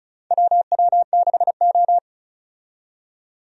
Morse code, heard as a keyed tone, about 34 wpm, 700 Hz. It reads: WW6O
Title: USA Call Signs - 4 Characters - 35wpm